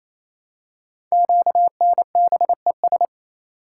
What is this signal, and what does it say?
Morse code, heard as a keyed tone, about 27 wpm, 700 Hz. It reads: QNBES
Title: Code Groups - 5 Letters - 28wpm